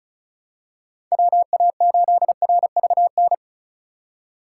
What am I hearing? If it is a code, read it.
Morse code, heard as a keyed tone, about 36 wpm, 700 Hz. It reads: WA8RVN
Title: USA Call Signs - 6 Characters - 4x Character Spacing - 35wpm